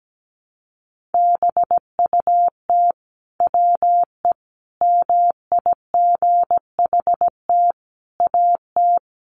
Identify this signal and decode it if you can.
Morse code, heard as a keyed tone, about 17 wpm, 700 Hz. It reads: BUT WE MIGHT AT
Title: Sentences from Top 1300 Words 17wpm